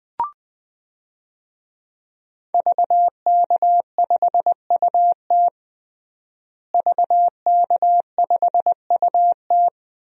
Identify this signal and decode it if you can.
Morse code, heard as a keyed tone, about 21 wpm, 700 Hz. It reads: VK5UT VK5UT
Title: DXCC-100 Call Signs - 5 Characters - Repeated 2x - 20wpm